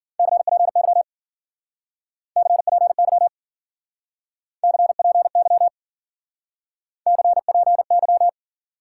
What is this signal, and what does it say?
Morse code, heard as a keyed tone, about 50 wpm, 700 Hz. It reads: CPY CPY CPY CPY
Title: Speed-Racing Top 90 QSO Elements 40wpm